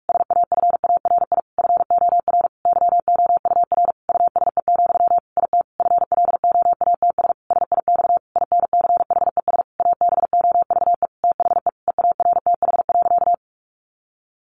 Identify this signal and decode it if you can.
Morse code, heard as a keyed tone, about 44 wpm, 700 Hz. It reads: SAFARI FOR YOUR USE, IT FLOATS SIX INCHES ABOVE THE EARTH.
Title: A Sound Of Thunder 45wpm